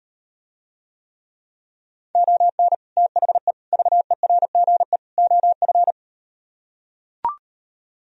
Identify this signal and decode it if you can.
Morse code, heard as a keyed tone, about 38 wpm, 700 Hz. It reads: ON THE VERGE OF